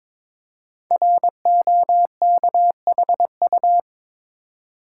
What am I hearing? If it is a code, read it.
Morse code, heard as a keyed tone, about 23 wpm, 700 Hz. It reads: ROKHU
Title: Code Groups - 5 Letters - 22wpm